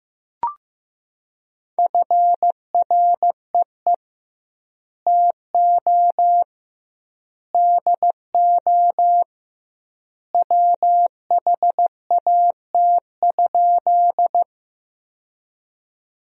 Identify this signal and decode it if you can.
Morse code, heard as a keyed tone, about 15 wpm, 700 Hz. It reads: FREE TO DO WHAT?
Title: 2x Word-Spacing - Sentences from Top 500 Words - 15wpm